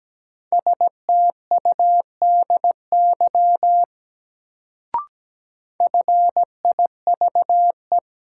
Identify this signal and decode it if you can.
Morse code, heard as a keyed tone, about 17 wpm, 700 Hz. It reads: STUDY FIVE
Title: Sets of 5 Words - New Word Drill - 17wpm